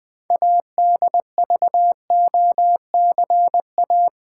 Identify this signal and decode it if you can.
Morse code, heard as a keyed tone, about 20 wpm, 700 Hz. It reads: ADVOCA
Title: Build A Word - 3 Letters At A Time - 8-Letter Words - 20wpm